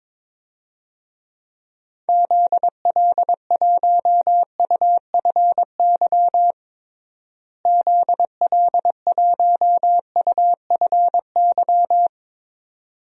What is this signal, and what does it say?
Morse code, heard as a keyed tone, about 22 wpm, 700 Hz. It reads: ZL1UFY ZL1UFY
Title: DXCC-100 Call Signs - 6 Characters - Repeated 2x - 22wpm